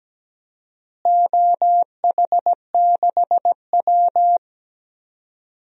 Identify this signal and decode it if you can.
Morse code, heard as a keyed tone, about 17 wpm, 700 Hz. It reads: OH6W
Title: DXCC-100 Call Signs - 4 Characters - Repeated 3x - 17wpm